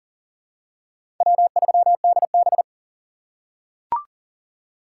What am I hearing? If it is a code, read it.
Morse code, heard as a keyed tone, about 40 wpm, 700 Hz. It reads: W3DB
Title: USA Call Signs - 4 Characters - 4x Character Spacing - 40wpm